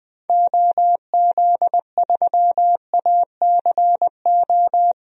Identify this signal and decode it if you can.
Morse code, heard as a keyed tone, about 20 wpm, 700 Hz. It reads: OZ3ACO
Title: DXCC-100 Call Signs - 6 Characters - Repeated 3x - 20wpm